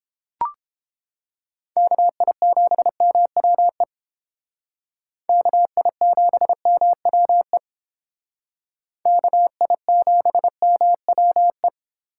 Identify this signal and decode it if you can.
Morse code, heard as a keyed tone, about 29 wpm, 700 Hz. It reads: KI7MWE KI7MWE KI7MWE